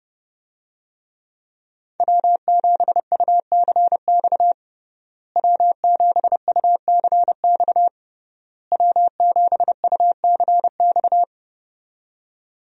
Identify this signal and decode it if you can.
Morse code, heard as a keyed tone, about 30 wpm, 700 Hz. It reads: W7UCX W7UCX W7UCX
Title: USA Call Signs - 5 Characters - Repeated 3x - 30wpm